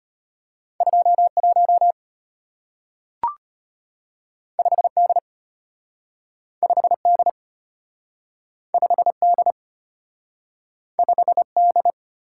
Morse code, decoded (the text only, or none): 21 5D 5D 5D 5D